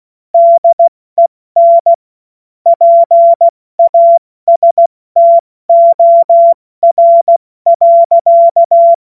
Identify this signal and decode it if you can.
Morse code, heard as a keyed tone, about 16 wpm, 680 Hz. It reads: DEN PASTOR.
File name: Witz-29-Juli-2025-70-F80